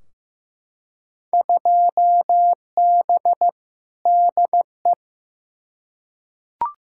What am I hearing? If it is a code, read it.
Morse code, heard as a keyed tone, about 15 wpm, 700 Hz. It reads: 2B DE
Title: Field Day Exchanges Encore - 15wpm